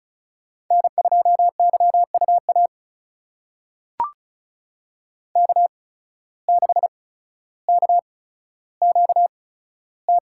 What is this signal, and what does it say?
Morse code, heard as a keyed tone, about 36 wpm, spaced as for 15 wpm, 700 Hz. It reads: N2YUA K6KQT
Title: USA Call Signs - 5 Characters - 8x Character Spacing - 35wpm